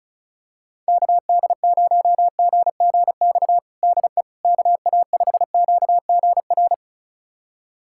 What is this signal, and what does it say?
Morse code, heard as a keyed tone, about 36 wpm, 700 Hz. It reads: KD0GGX DE KA5QGR
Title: USA Callsign DE Callsign 35wpm